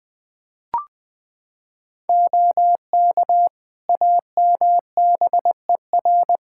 Morse code, heard as a keyed tone, about 20 wpm, 700 Hz. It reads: OK AMBER
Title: Practice QSO - Exchange 3 - More Info - 20wpm